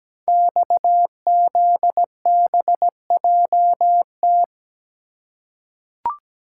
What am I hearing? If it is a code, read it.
Morse code, heard as a keyed tone, about 17 wpm, 700 Hz. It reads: XZBJT
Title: Code Groups - 5 Letter:Numbers - 17wpm